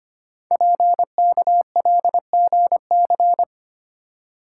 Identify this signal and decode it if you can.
Morse code, heard as a keyed tone, about 25 wpm, 700 Hz. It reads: PKLGC
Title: Code Groups - 5 Letter:Numbers - 25wpm